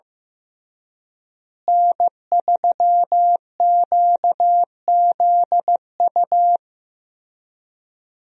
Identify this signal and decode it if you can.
Morse code, heard as a keyed tone, about 16 wpm, 700 Hz. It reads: N3QZU
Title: USA Call Signs - 5 Characters - Repeated 3x - 15wpm